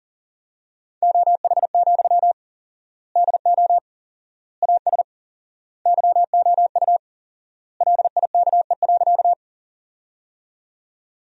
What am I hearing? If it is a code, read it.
Morse code, heard as a keyed tone, about 40 wpm, 700 Hz. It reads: OH, DO AS YOU LIKE.
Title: 4x Word-Spacing - Sentences from Top 500 Words - 40wpm